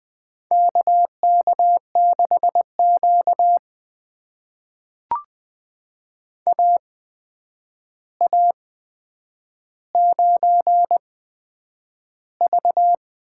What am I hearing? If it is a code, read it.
Morse code, heard as a keyed tone, about 20 wpm, spaced as for 9 wpm, 700 Hz. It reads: KK6Q AA9V
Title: USA Call Signs - 4 Characters - 8x Character Spacing - 20wpm